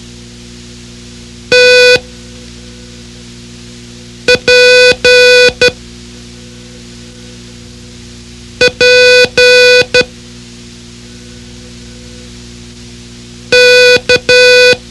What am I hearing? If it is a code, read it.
Morse code, heard as a keyed tone, about 12 wpm, 500 Hz. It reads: T P P K
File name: ham radio / CodeQuick / Code Quick 04